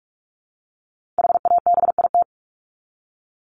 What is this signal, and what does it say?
Morse code, heard as a keyed tone, about 44 wpm, 700 Hz. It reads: HABIT
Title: Top 1751-2000 Words 45wpm